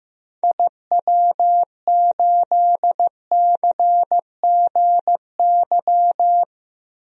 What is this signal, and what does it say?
Morse code, heard as a keyed tone, about 16 wpm, 700 Hz. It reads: IW8CGY
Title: DXCC-100 Call Signs - 6 Characters - Repeated 3x - 15wpm